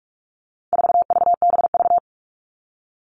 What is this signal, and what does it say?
Morse code, heard as a keyed tone, about 44 wpm, 700 Hz. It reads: 4VBV
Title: All Cut Numbers - 4-Digit 45wpm